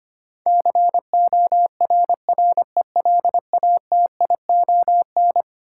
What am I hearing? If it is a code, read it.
Morse code, heard as a keyed tone, about 25 wpm, 700 Hz. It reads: CORRELATION
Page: morse-code-ninja-practice-25wpm